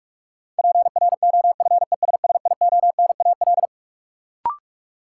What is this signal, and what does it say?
Morse code, heard as a keyed tone, about 46 wpm, 700 Hz. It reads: PROFESSIONAL
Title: Sentences from Top 1000 Words - Review Part 4 - New Word Drill - 45wpm